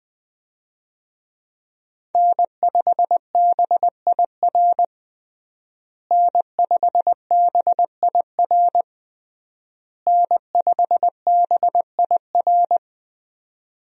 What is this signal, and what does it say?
Morse code, heard as a keyed tone, about 20 wpm, 700 Hz. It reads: N5BIR N5BIR N5BIR
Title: USA Call Signs - 5 Characters - Repeated 3x - 20wpm